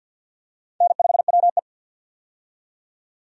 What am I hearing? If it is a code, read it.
Morse code, heard as a keyed tone, about 50 wpm, 700 Hz. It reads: N5WE
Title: USA Call Signs - 4 Characters - 50wpm